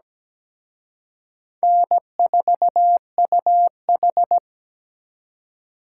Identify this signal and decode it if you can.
Morse code, heard as a keyed tone, about 20 wpm, 700 Hz. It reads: N4UH
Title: Speed Racing - USA 1x2 Call Signs 17wpm